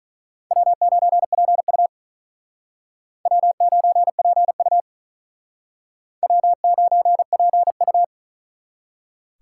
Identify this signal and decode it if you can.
Morse code, heard as a keyed tone, about 41 wpm, 700 Hz. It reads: W9PU W9PU W9PU
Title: Speed Racing - USA 1x2 Call Signs 35wpm